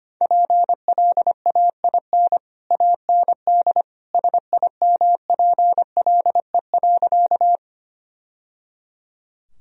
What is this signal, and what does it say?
Morse code, heard as a keyed tone, about 25 wpm, 700 Hz. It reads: PLAIN AND SIMPLE.
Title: Sentences from Top 1000 Words - Review Part 2 - 25wpm